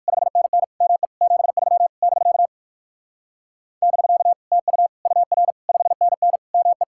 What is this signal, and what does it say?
Morse code, heard as a keyed tone, about 62 wpm, 700 Hz. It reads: 5NN ME 73 <BK> <BK> TU UR 5NN ME